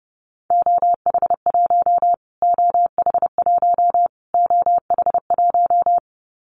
Morse code, heard as a keyed tone, about 28 wpm, 700 Hz. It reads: OH1 OH1 OH1
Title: Callsign Prefixes for Top 25 DXCC - Repeated 3x - 30wpm